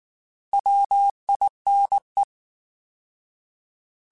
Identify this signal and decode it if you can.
Morse code, heard as a keyed tone, about 19 wpm, 800 Hz. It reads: WINE
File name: Alphabetical / Alphabetical mp3s / Ws @18